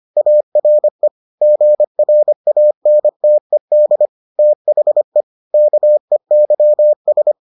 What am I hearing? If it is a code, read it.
Morse code, heard as a keyed tone, about 25 wpm, 600 Hz. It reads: ARE GRANTED THE KEYS